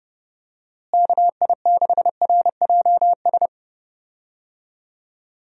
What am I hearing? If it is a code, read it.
Morse code, heard as a keyed tone, about 30 wpm, 700 Hz. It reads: KI6RJS
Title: USA Call Signs - 6 Characters - Repeated 2x - 30wpm